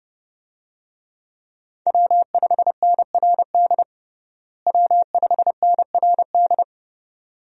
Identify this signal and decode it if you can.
Morse code, heard as a keyed tone, about 32 wpm, 700 Hz. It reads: W5NRD W5NRD
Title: USA Call Signs - 5 Characters - Repeated 2x - 30wpm